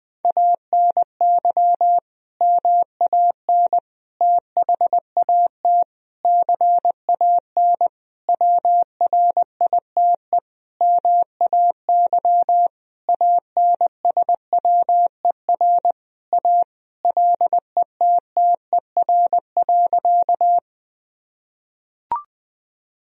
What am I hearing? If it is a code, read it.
Morse code, heard as a keyed tone, about 21 wpm, 700 Hz. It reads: ANY MAN THAT CAN WRITE MAY ANSWER A LETTER.
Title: Sentences from Top 1000 Words - Review Part 1 - 20wpm